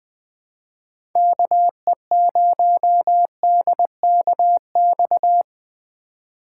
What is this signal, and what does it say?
Morse code, heard as a keyed tone, about 20 wpm, 700 Hz. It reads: KE0DKX
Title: USA Call Signs - 6 Characters - 4x Character Spacing - 20wpm